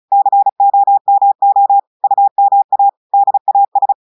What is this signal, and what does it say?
Morse code, heard as a keyed tone, about 36 wpm, 800 Hz. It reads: COMO UMA DAS
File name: audios / portugais / portugais texte en clair 1 vitesse 35 mots minutes